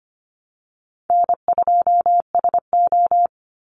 Morse code, heard as a keyed tone, about 25 wpm, 700 Hz. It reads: N2SO
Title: Call Signs Repeated 2x - 25wpm